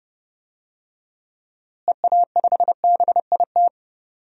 Morse code, heard as a keyed tone, about 30 wpm, 700 Hz. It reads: EA5BIT
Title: DXCC-100 Call Signs - 6 Characters - Repeated 2x - 30wpm